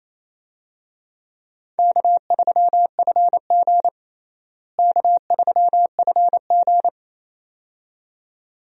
Morse code, heard as a keyed tone, about 28 wpm, 700 Hz. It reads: K3FG K3FG
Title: USA Call Signs - 4 Characters - Repeated 2x - 28wpm